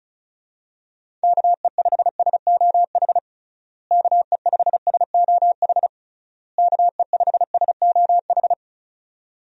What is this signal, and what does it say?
Morse code, heard as a keyed tone, about 34 wpm, 700 Hz. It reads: KE5SOH KE5SOH KE5SOH
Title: USA Call Signs - 6 Characters - Repeated 3x - 35wpm